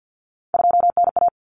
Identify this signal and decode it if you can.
Morse code, heard as a keyed tone, about 50 wpm, 700 Hz. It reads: 2NA
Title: Common Cut Numbers - 3-Digit 50wpm